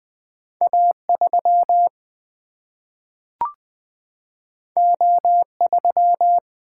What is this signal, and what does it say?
Morse code, heard as a keyed tone, about 20 wpm, 700 Hz. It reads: A 3 O 3
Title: Single-Letter Single-Number 20wpm